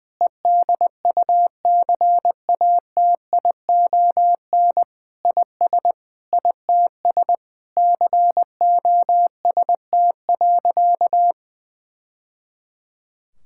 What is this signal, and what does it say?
Morse code, heard as a keyed tone, about 20 wpm, 700 Hz. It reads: EDUCATION IS ITS COST.
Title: Sentences from Top 1000 Words - Review Part 6 - 20wpm